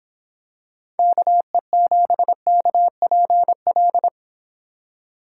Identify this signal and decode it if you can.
Morse code, heard as a keyed tone, about 26 wpm, 700 Hz. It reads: KE7KPL